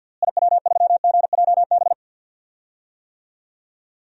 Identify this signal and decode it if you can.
Morse code, heard as a keyed tone, about 48 wpm, 700 Hz. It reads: IW3GJB
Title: DXCC-100 Call Signs - 6 Characters - Repeated 2x - 50wpm